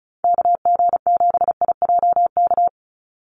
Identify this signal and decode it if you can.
Morse code, heard as a keyed tone, about 34 wpm, 700 Hz. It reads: KG7IJK
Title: USA Call Signs 35wpm